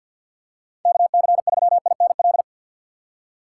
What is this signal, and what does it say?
Morse code, heard as a keyed tone, about 52 wpm, 700 Hz. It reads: KC3INL
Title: USA Call Signs - 6 Characters - 6x Character Spacing - 50wpm